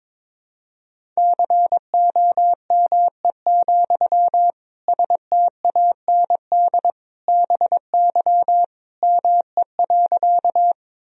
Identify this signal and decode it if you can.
Morse code, heard as a keyed tone, about 22 wpm, 700 Hz. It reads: COME, STAND BY ME.
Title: Sentences from Top 1000 Words - Review Part 1 - 22wpm